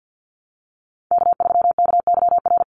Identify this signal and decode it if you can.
Morse code, heard as a keyed tone, about 50 wpm, 700 Hz. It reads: K3KYR
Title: NA SOTA Chaser Callsigns 50wpm